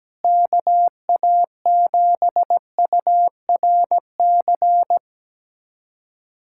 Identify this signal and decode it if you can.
Morse code, heard as a keyed tone, about 17 wpm, 700 Hz. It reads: KA7URC
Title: USA Callsign DE Callsign - 6 Characters - 17wpm